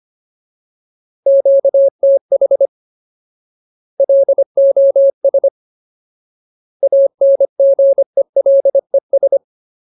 Morse code, heard as a keyed tone, about 25 wpm, 550 Hz. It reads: QTH LOS ANGELES
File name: Ultimate Word List / 25WPM Ultimate